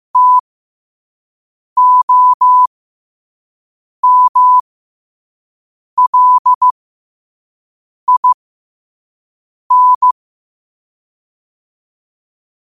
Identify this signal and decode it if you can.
Morse code, heard as a keyed tone, about 15 wpm, spaced as for 5 wpm, 1000 Hz. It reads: TOMLIN
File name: qotdfiles / 5wpm